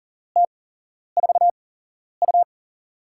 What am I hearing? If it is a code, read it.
Morse code, heard as a keyed tone, about 40 wpm, spaced as for 11 wpm, 700 Hz. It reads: T4U
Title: USA Call Signs - 8x Character Spacing - 40wpm